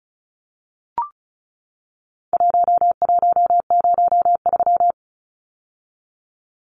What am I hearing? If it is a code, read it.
Morse code, heard as a keyed tone, about 34 wpm, 700 Hz. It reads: 1103
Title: Numbers - Contest Style - 4-Digit 35wpm